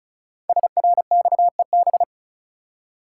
Morse code, heard as a keyed tone, about 35 wpm, 700 Hz. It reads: SRXEB
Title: Code Groups - 5 Letters - 35wpm